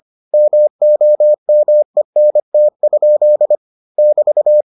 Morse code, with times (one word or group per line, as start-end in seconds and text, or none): 0.3-3.5 MOMENT?
4.0-4.6 <BT>